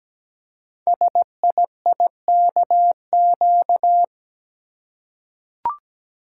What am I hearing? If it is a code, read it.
Morse code, heard as a keyed tone, about 17 wpm, 700 Hz. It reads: SIIKQ
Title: Code Groups - 5 Letters - 17wpm